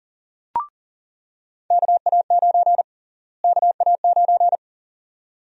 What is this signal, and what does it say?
Morse code, heard as a keyed tone, about 40 wpm, 700 Hz. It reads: KA9 KA9
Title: USA Call Sign Prefixes - Repeated 2x - 40wpm